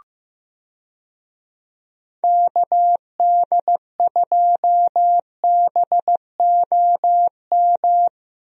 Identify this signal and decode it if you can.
Morse code, heard as a keyed tone, about 15 wpm, 700 Hz. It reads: KD2BOM
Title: USA Call Signs - 6 Characters - Repeated 3x - 15wpm